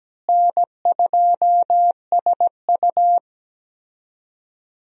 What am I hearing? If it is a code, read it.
Morse code, heard as a keyed tone, about 17 wpm, 700 Hz. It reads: N2SU
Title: USA Call Signs - 4 Characters - Repeated 3x - 17wpm